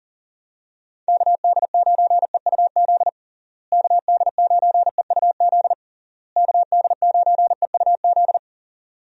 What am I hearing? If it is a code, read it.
Morse code, heard as a keyed tone, about 40 wpm, 700 Hz. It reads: KD9EUZ KD9EUZ KD9EUZ
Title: USA Call Signs - 6 Characters - Repeated 3x - 40wpm